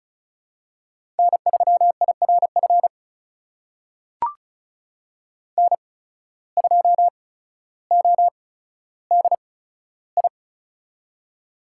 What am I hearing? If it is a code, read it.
Morse code, heard as a keyed tone, about 34 wpm, spaced as for 15 wpm, 690 Hz. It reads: N3IRF N2ODI
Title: USA Call Signs - 5 Characters - 8x Character Spacing - 35wpm